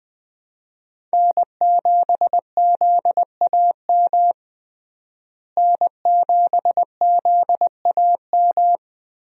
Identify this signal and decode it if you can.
Morse code, heard as a keyed tone, about 20 wpm, 700 Hz. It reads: N7ZAM N7ZAM
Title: USA Call Signs - 5 Characters - Repeated 3x - 20wpm